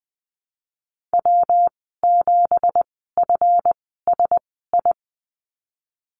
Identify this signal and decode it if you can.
Morse code, heard as a keyed tone, about 20 wpm, spaced as for 15 wpm, 700 Hz. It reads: W7FSI
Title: 200 Lesson 32g - Letter k - Call Signs 20wpm